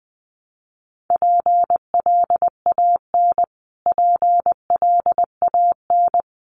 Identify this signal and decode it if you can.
Morse code, heard as a keyed tone, about 20 wpm, 700 Hz. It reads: PLAN PLAN
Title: Top 500 Words Repeated 20wpm